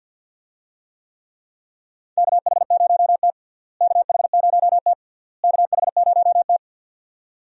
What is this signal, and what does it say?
Morse code, heard as a keyed tone, about 51 wpm, 700 Hz. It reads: KH0T KH0T KH0T
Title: USA Call Signs - 4 Characters - Repeated 3x - 50wpm